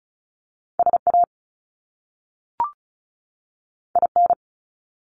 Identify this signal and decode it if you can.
Morse code, heard as a keyed tone, about 36 wpm, 700 Hz. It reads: SA IN
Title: Top Two Letter Combinations in Words 35wpm